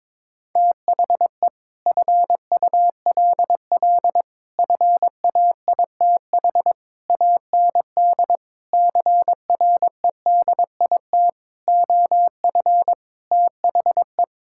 Morse code, heard as a keyed tone, about 22 wpm, 700 Hz. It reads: THE FULL FAITH AND CREDIT OF THE
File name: Sets of 9 Words - 22wpm